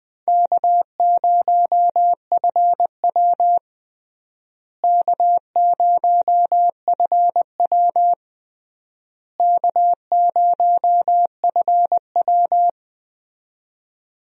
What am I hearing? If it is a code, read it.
Morse code, heard as a keyed tone, about 20 wpm, 700 Hz. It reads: K0FW K0FW K0FW
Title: USA Call Signs - 4 Characters - Repeated 3x - 20wpm